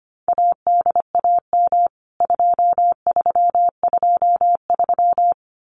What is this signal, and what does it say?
Morse code, heard as a keyed tone, about 25 wpm, 700 Hz. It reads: ADAM 2323
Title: CWT Exchange 25wpm